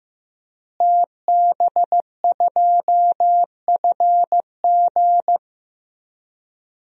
Morse code, heard as a keyed tone, about 15 wpm, 700 Hz. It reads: TB2FG
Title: DXCC-100 Call Signs - 5 Characters - Repeated 3x - 15wpm